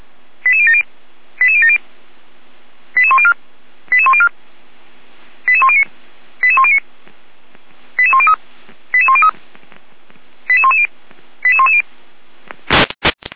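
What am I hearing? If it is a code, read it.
Five-tone selective call (ZVEI), heard as quick runs of five tones, tones of about 70 ms. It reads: 88070 88070 88104 88104 88109 88109 88103 88103 88110 88110
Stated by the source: ZVEI five-tone call 88283 received by SDR.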